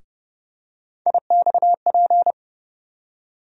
Morse code, heard as a keyed tone, about 30 wpm, 700 Hz. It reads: IXP
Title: USA Call Sign Postfixes - 3 Letters 30wpm